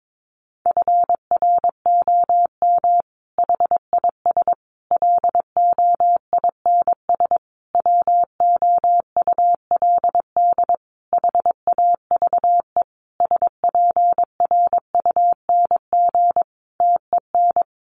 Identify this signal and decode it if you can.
Morse code, heard as a keyed tone, about 22 wpm, 700 Hz. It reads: FROM HIS LOINS WOULD HAVE SPRUNG TEN
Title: A Sound Of Thunder 22wpm